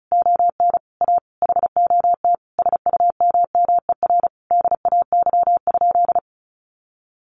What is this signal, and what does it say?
Morse code, heard as a keyed tone, about 34 wpm, 700 Hz. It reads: ON A HOT SUMMER DAY?
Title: Sentences from Top 1500 Words 35wpm